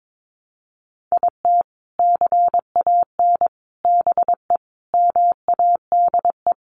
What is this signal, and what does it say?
Morse code, heard as a keyed tone, about 22 wpm, 700 Hz. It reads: IT CAN BE MADE
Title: Sentences from Top 1200 Words 22wpm